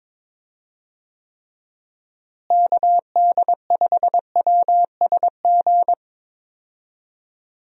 Morse code, heard as a keyed tone, about 22 wpm, 700 Hz. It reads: KD5WSG
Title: USA Call Signs - 6 Characters - Repeated 3x - 22wpm